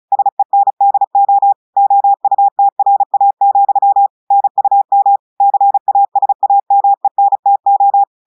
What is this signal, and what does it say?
Morse code, heard as a keyed tone, about 36 wpm, 800 Hz. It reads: SENDO OUTRA, NUM CASAMENTO
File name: audios / portugais / portugais texte en clair 1 vitesse 35 mots minutes